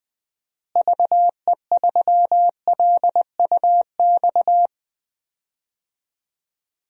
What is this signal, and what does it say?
Morse code, heard as a keyed tone, about 20 wpm, 700 Hz. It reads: VE3LUX
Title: DXCC-100 Call Signs - 6 Characters - Repeated 3x - 20wpm